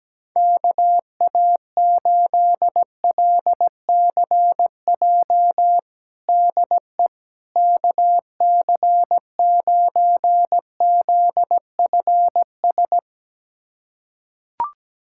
Morse code, heard as a keyed tone, about 17 wpm, 700 Hz. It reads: KA8LCJ DE KC9ZFS
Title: USA Callsign DE Callsign - 6 Characters - 17wpm